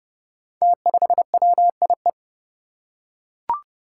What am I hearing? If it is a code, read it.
Morse code, heard as a keyed tone, about 31 wpm, 700 Hz. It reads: T5WIE
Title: Five Challenging Letter-Number Combinations 30wpm